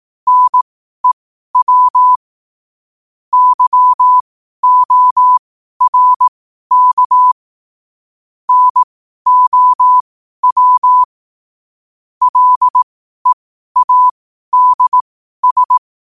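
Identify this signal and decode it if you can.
Morse code, heard as a keyed tone, about 18 wpm, spaced as for 12 wpm, 1000 Hz. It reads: NEW YORK NOW LEADS
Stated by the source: 12 wpm Morse.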